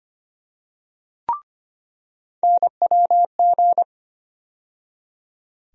Morse code, heard as a keyed tone, about 25 wpm, 700 Hz. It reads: NWG
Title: USA Call Sign Postfixes - 3 Letters 25wpm